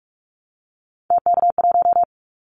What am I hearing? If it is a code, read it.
Morse code, heard as a keyed tone, about 44 wpm, 700 Hz. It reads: TK1
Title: Callsign Prefixes for Top 100 DXCC - 45wpm